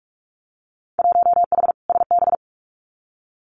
Morse code, heard as a keyed tone, about 44 wpm, 700 Hz. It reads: 1H SB
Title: Winter Field Day Exchanges 45wpm